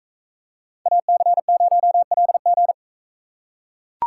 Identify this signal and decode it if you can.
Morse code, heard as a keyed tone, about 42 wpm, 700 Hz. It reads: AC0LG